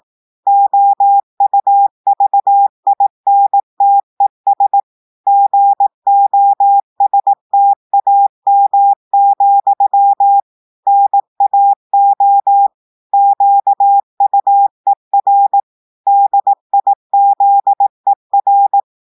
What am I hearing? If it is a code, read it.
Morse code, heard as a keyed tone, about 18 wpm, 800 Hz. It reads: OUVINTES GOSTAM, NAO QUER DIZER
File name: portugais texte en clair 2 vitesse 18 mots minutes